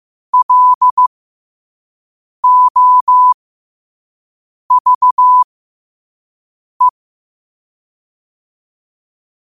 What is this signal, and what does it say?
Morse code, heard as a keyed tone, about 15 wpm, spaced as for 5 wpm, 1000 Hz. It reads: LOVE
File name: qotdfiles / 5wpm